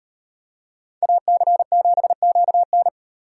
Morse code, heard as a keyed tone, about 32 wpm, 700 Hz. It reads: AC7QN